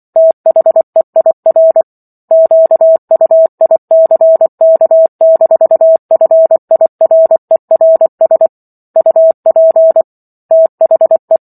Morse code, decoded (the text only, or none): THEIR QUICK-FIRERS UP THE